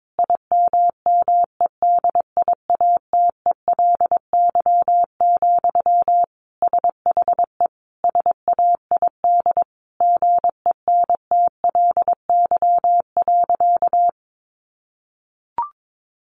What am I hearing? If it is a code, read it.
Morse code, heard as a keyed tone, about 22 wpm, 700 Hz. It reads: IMMEDIATELY, SHE SAID GENTLY.
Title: Sentences from Top 1500 Words 22wpm